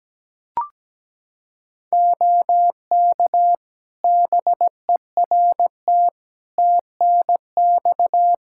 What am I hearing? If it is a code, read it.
Morse code, heard as a keyed tone, about 17 wpm, 700 Hz. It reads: OK BERT TNX
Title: Practice QSO - Exchange 4 - Ending - 17wpm